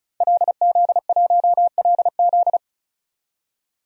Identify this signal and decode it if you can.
Morse code, heard as a keyed tone, about 34 wpm, 700 Hz. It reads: LZ1LZ
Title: DXCC-100 Call Signs - 5 Characters - 35wpm